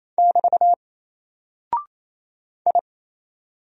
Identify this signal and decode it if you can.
Morse code, heard as a keyed tone, about 27 wpm, 700 Hz. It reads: <BT> I
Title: Single Character (Letter:Number:Punctuation:Prosign) 28wpm